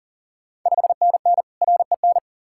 Morse code, heard as a keyed tone, about 42 wpm, 690 Hz. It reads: 5NN REN
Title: Ontario QSO Party Exchange 40wpm